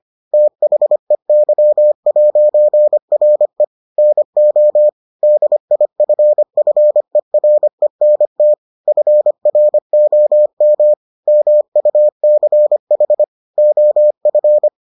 Morse code, heard as a keyed tone, about 25 wpm, 600 Hz. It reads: THEY'RE NO DIFFERENT FROM MUCH OF